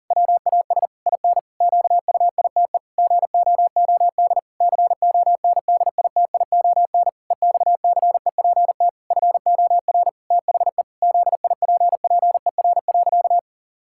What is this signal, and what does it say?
Morse code, heard as a keyed tone, about 40 wpm, 700 Hz. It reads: WAS IN QUITE GOOD CONDITION EXCEPT FOR THE ZIPPER.
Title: Pangram Challenge v2 - 40wpm